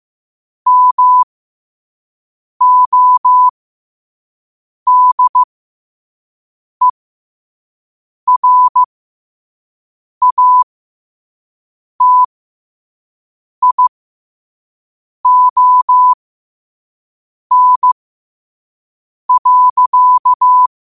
Morse code, heard as a keyed tone, about 15 wpm, spaced as for 5 wpm, 1000 Hz. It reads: MODERATION.
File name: 5wpm